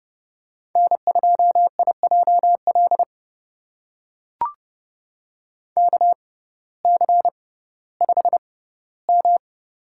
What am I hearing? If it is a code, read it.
Morse code, heard as a keyed tone, about 30 wpm, spaced as for 17 wpm, 700 Hz. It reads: N2IJL KC5M
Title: USA Call Signs - 6x Character Spacing - 30wpm